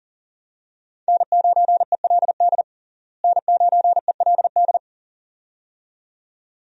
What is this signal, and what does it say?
Morse code, heard as a keyed tone, about 40 wpm, 700 Hz. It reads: N9ELD N9ELD
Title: USA Call Signs - 5 Characters - Repeated 2x - 40wpm